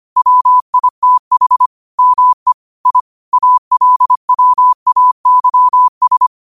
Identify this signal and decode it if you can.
Morse code, heard as a keyed tone, about 25 wpm, 1000 Hz. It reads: WITH ME I ALWAYS
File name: qotdfiles / 25wpm